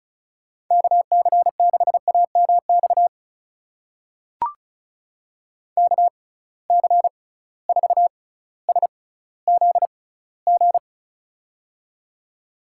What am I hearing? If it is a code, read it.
Morse code, heard as a keyed tone, about 36 wpm, spaced as for 18 wpm, 700 Hz. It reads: KC6AMX KC4SZG
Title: USA Call Signs - 6 Characters - 6x Character Spacing - 35wpm